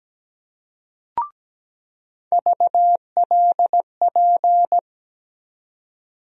Morse code, heard as a keyed tone, about 17 wpm, 700 Hz. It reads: VLP
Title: USA Call Sign Postfixes - 3 Letters 17wpm